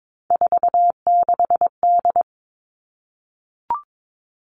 Morse code, heard as a keyed tone, about 22 wpm, 700 Hz. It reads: 46D
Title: All Cut Numbers - 3-Digit 22wpm